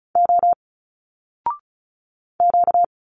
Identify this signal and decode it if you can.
Morse code, heard as a keyed tone, about 35 wpm, 700 Hz. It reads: O Q
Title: Single Letters 35wpm